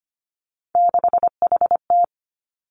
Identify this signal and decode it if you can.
Morse code, heard as a keyed tone, about 25 wpm, 700 Hz. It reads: 6HT
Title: Three Challenging Letter-Number Combinations 25wpm